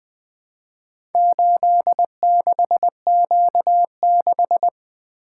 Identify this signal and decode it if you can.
Morse code, heard as a keyed tone, about 20 wpm, 700 Hz. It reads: 86Q6
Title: Four Letter-Number Combinations - 20wpm